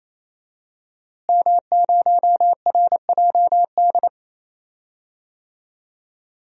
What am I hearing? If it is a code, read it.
Morse code, heard as a keyed tone, about 28 wpm, 700 Hz. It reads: M0RJD
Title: DXCC-100 Call Signs - 5 Characters - Repeated 3x - 28wpm